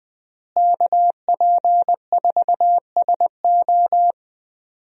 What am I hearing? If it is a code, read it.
Morse code, heard as a keyed tone, about 20 wpm, 700 Hz. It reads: KP4SO
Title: DXCC-100 Call Signs - 5 Characters - Repeated 2x - 20wpm